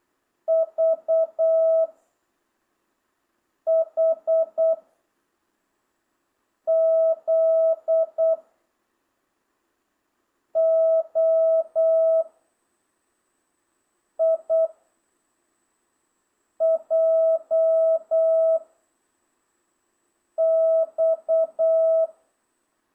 Morse code, heard as a keyed tone, about 8 wpm, 660 Hz. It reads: V H Z O I J X